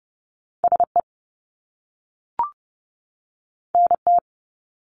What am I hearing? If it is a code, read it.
Morse code, heard as a keyed tone, about 30 wpm, 700 Hz. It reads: SE NT
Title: Top Two Letter Combinations in Words 30wpm